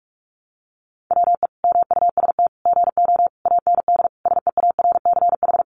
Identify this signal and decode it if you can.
Morse code, heard as a keyed tone, about 44 wpm, 700 Hz. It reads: WE MUST GO AND SEARCH
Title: Sentences from Top 800 Words 45wpm